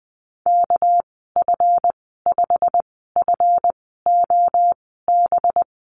Morse code, heard as a keyed tone, about 20 wpm, spaced as for 14 wpm, 700 Hz. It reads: KF5FOB
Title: 213 Lesson 34g - Number 8 - Call Signs 20wpm